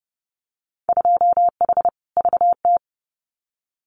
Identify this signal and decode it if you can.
Morse code, heard as a keyed tone, about 30 wpm, 700 Hz. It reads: 2H VT
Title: Winter Field Day Exchanges 30wpm